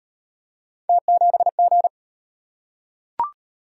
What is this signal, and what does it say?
Morse code, heard as a keyed tone, about 34 wpm, 700 Hz. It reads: T7G